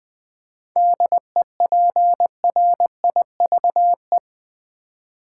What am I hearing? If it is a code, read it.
Morse code, heard as a keyed tone, about 20 wpm, 700 Hz. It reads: DEPRIVE
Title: 7 Letter Words - First 3 Letters Given - 20wpm